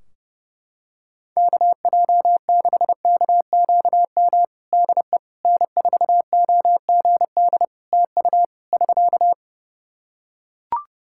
Morse code, heard as a keyed tone, about 30 wpm, 700 Hz. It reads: KJ6KQM DE N4OGD TU <SK>
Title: Practice QSO - Exchange 4 - Ending - 30wpm